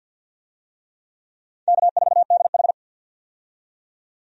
Morse code, heard as a keyed tone, about 48 wpm, 700 Hz. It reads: K4DH
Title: USA Call Signs - 4 Characters - 50wpm